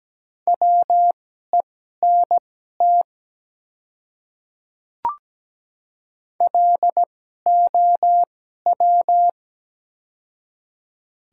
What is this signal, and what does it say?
Morse code, heard as a keyed tone, about 17 wpm, spaced as for 12 wpm, 700 Hz. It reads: WENT LOW
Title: 2x Character Spacing - Top 200 Words - 17wpm